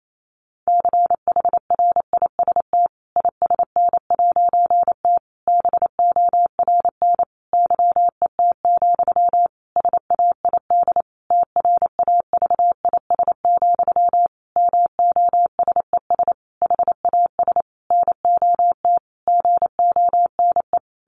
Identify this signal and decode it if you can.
Morse code, heard as a keyed tone, about 28 wpm, 700 Hz. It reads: CHRIST ISN'T BORN YET, SAID TRAVIS, MOSES HAS NOT GONE
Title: A Sound Of Thunder 28wpm